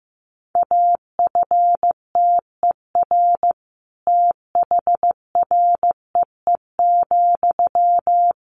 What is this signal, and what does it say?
Morse code, heard as a keyed tone, about 15 wpm, 700 Hz. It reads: AFTER THREE,
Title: Sentences from Top 200 Words 15wpm